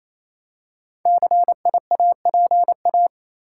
Morse code, heard as a keyed tone, about 27 wpm, 690 Hz. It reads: CIAPA
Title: Code Groups - 5 Letters - 28wpm